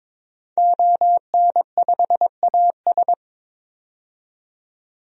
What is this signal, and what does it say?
Morse code, heard as a keyed tone, about 23 wpm, 700 Hz. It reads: ON5AS
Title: DXCC-100 Call Signs - 5 Characters - 22wpm